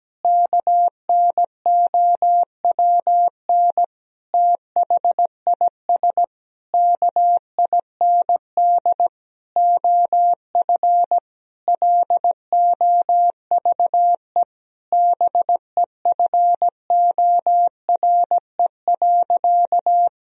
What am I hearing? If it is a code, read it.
Morse code, heard as a keyed tone, about 17 wpm, 700 Hz. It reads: KNOWN THIS KIND OF LOVE BEFORE.
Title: Sentences from Top 1000 Words - Review Part 3 - 17wpm